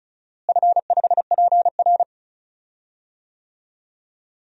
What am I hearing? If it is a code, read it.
Morse code, heard as a keyed tone, about 36 wpm, 700 Hz. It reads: F5PR
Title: DXCC-100 Call Signs - 4 Characters - Repeated 3x - 35wpm